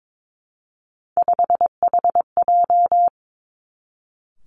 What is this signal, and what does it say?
Morse code, heard as a keyed tone, about 22 wpm, 700 Hz. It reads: 5HJ
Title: Three Challenging Letter-Number Combinations 22wpm